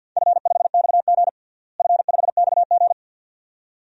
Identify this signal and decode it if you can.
Morse code, heard as a keyed tone, about 50 wpm, 700 Hz. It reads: F5XG F5XG
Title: DXCC-100 Call Signs - 4 Characters - Repeated 2x - 50wpm